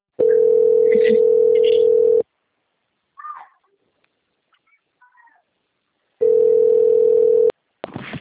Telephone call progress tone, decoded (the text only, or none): ringback tone